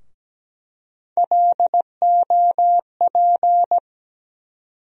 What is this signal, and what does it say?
Morse code, heard as a keyed tone, about 17 wpm, 700 Hz. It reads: LOP
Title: USA Call Sign Postfixes - 3 Letters 17wpm